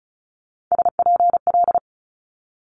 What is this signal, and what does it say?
Morse code, heard as a keyed tone, about 36 wpm, 700 Hz. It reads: SPL
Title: Multi-letter Phonemes Quick Fire 35wpm